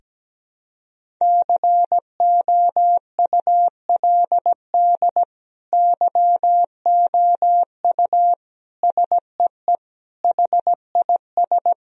Morse code, heard as a keyed tone, about 17 wpm, 700 Hz. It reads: COULD YOU SEE HIS
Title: Sentences from Top 1000 Words - Review Part 2 - 17wpm